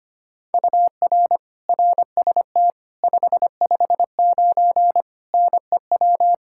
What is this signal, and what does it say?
Morse code, heard as a keyed tone, about 25 wpm, 700 Hz. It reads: UR RST 559 NEW
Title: Practice QSO - Exchange 1 - Beginning - 25wpm